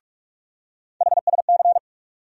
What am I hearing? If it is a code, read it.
Morse code, heard as a keyed tone, about 42 wpm, 700 Hz. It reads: HSC
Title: USA Call Sign Postfixes - 3 Letters 45wpm